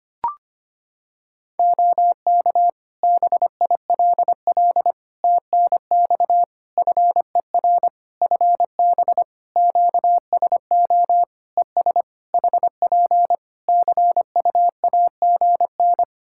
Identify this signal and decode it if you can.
Morse code, heard as a keyed tone, about 25 wpm, 700 Hz. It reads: OK BILL TNX FER FB QSO ES HP CUAGN
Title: Practice QSO - Exchange 4 - Ending - 25wpm